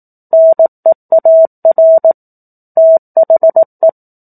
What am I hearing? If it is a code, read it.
Morse code, heard as a keyed tone, about 18 wpm, 650 Hz. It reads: NEAR THE
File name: 18 WPM